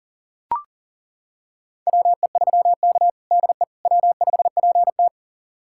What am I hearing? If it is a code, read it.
Morse code, heard as a keyed tone, about 40 wpm, 700 Hz. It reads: WE3K DE W5PT
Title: USA Callsign DE Callsign - 4 Characters - 40wpm